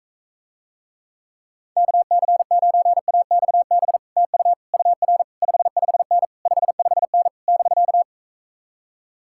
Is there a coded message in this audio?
Yes: Morse code from 1.8 to 8.0 s, heard as a keyed tone, about 45 wpm, 700 Hz. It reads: KC9AXB TU UR 55N 55N <BK>